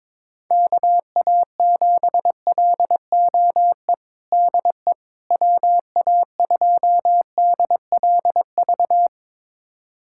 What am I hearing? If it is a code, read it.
Morse code, heard as a keyed tone, about 22 wpm, 700 Hz. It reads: KA7LOE DE WA2DLV
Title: USA Callsign DE Callsign - 6 Characters - 22wpm